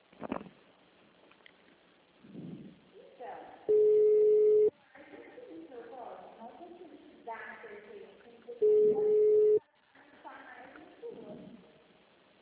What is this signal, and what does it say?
Ringback tone (Europe, 425 Hz), with speech.